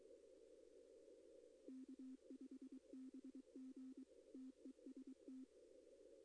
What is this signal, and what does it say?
Morse code, heard as a keyed tone, about 22 wpm, 260 Hz. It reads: K5BG TEST
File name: contests / arrl-160-k5tr-op-w5tn -2200